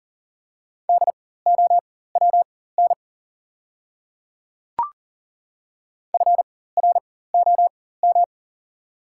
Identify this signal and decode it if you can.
Morse code, heard as a keyed tone, about 40 wpm, spaced as for 19 wpm, 700 Hz. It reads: DOWN FROM
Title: 4x Character Spacing - Top 100 Words - 40wpm